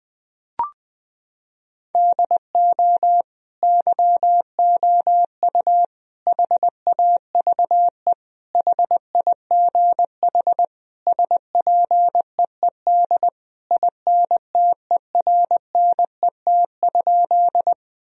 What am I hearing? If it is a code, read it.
Morse code, heard as a keyed tone, about 20 wpm, 700 Hz. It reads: DO YOU HAVE HIGH SPEED INTERNET?
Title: Sentences from Top 1000 Words - Review Part 4 - 20wpm